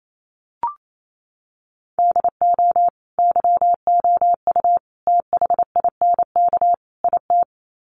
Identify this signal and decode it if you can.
Morse code, heard as a keyed tone, about 28 wpm, 700 Hz. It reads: DO YOU THINK IT
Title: Sets of 4 Words 28wpm